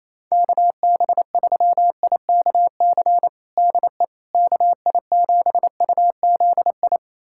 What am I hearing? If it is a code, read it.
Morse code, heard as a keyed tone, about 28 wpm, 700 Hz. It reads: KB3IKC DE KI7UZI
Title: USA Callsign DE Callsign - 6 Characters - 28wpm